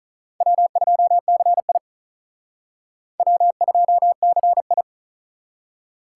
Morse code, heard as a keyed tone, about 45 wpm, 700 Hz. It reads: W2CI W2CI